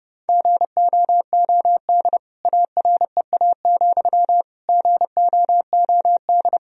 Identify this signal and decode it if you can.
Morse code, heard as a keyed tone, about 32 wpm, 700 Hz. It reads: GOOD AREA, GOOD
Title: Sentences from Top 1000 Words - Review Part 2 - 30wpm